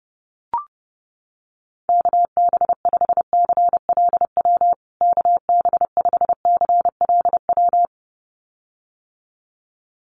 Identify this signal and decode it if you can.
Morse code, heard as a keyed tone, about 30 wpm, 700 Hz. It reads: KB5CLW KB5CLW
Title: USA Call Signs Repeated 2x 30wpm